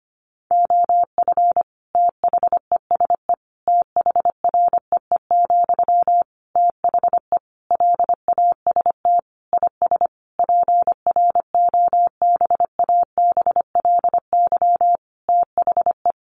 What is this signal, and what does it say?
Morse code, heard as a keyed tone, about 25 wpm, 700 Hz. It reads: OF THESE THREE, THE LAST IS PROBABLY THE
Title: Sentences from Top 700 Words 25wpm